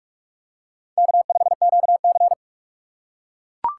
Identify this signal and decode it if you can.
Morse code, heard as a keyed tone, about 40 wpm, 700 Hz. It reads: K5QC